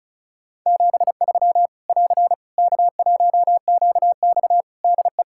Morse code, heard as a keyed tone, about 36 wpm, 700 Hz. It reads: 73 <AR> K1QX DE
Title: Practice QSO - Exchange 4 - Ending - 35wpm